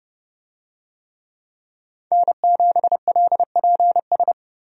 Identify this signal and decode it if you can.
Morse code, heard as a keyed tone, about 31 wpm, 700 Hz. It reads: N7LPS
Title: USA Call Signs - 5 Characters - Repeated 3x - 30wpm